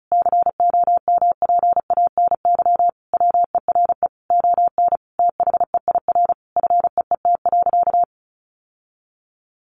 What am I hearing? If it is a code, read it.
Morse code, heard as a keyed tone, about 34 wpm, 700 Hz. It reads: COMPANY WERE ON THEIR FEET.
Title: Sentences from Top 1000 Words 35wpm